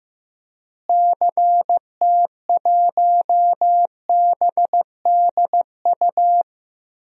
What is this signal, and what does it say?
Morse code, heard as a keyed tone, about 16 wpm, 700 Hz. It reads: CT1BDU
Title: DXCC-100 Call Signs - 6 Characters - Repeated 2x - 15wpm